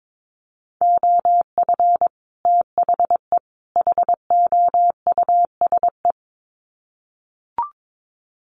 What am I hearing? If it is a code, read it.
Morse code, heard as a keyed tone, about 22 wpm, 700 Hz. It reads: OF THE HOUSE
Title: Sets of 3 Words 22wpm